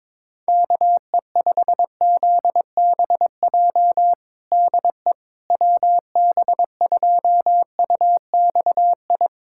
Morse code, heard as a keyed tone, about 22 wpm, 700 Hz. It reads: KE5ZBJ DE WB2UXI
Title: USA Callsign DE Callsign 22wpm